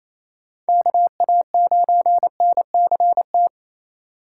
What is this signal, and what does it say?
Morse code, heard as a keyed tone, about 28 wpm, 700 Hz. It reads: KA9NCT
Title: USA Call Signs - 6 Characters - 6x Character Spacing - 28wpm